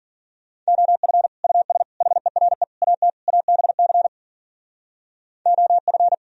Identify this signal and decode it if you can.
Morse code, heard as a keyed tone, about 53 wpm, 700 Hz. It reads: OF US HERE AT ABC OF